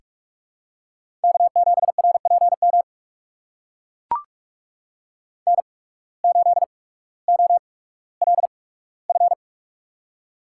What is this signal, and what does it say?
Morse code, heard as a keyed tone, about 43 wpm, spaced as for 19 wpm, 700 Hz. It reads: K7RPM N8OLF
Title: USA Call Signs - 5 Characters - 8x Character Spacing - 45wpm